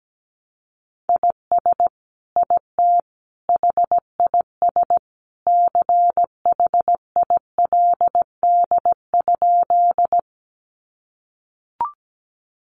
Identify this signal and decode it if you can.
Morse code, heard as a keyed tone, about 17 wpm, 700 Hz. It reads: IS IT HIS CHILD?
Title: Sentences from Top 800 Words 17wpm